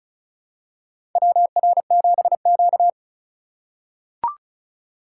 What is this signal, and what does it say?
Morse code, heard as a keyed tone, about 34 wpm, 700 Hz. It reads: WR7Q
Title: USA Call Signs - 4 Characters - 2x Character Spacing - 35wpm